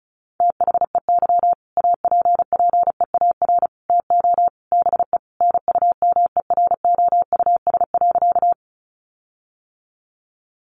Morse code, heard as a keyed tone, about 34 wpm, 700 Hz. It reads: THEY APPEAR TO BE NUMEROUS.
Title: Sentences from Top 1400 Words 35wpm